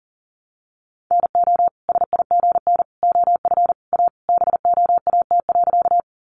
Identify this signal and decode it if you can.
Morse code, heard as a keyed tone, about 40 wpm, 700 Hz. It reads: NO SIGN OF A BOAT.
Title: Sentences from Top 1000 Words 40wpm